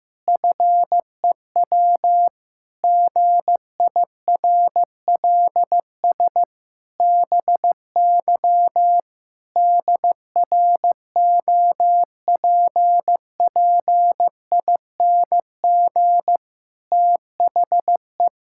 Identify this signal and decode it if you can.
Morse code, heard as a keyed tone, about 16 wpm, 700 Hz. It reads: FEW GIRLS BY DROPPING THE
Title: Pangram Challenge v2 - 15wpm